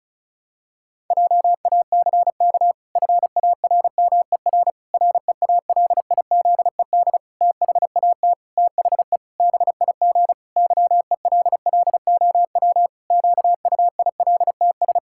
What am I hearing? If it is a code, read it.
Morse code, heard as a keyed tone, about 34 wpm, 700 Hz. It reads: JACK FARMER REALIZED THAT THE BIG YELLOW QUILTS
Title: Pangram Challenge v2 - 35wpm